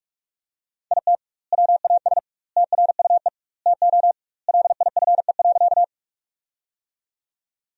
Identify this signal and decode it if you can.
Morse code, heard as a keyed tone, about 46 wpm, 700 Hz. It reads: IT WAS TRUE TO LIFE.
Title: 2x Word-Spacing - Sentences from Top 400 Words - 45wpm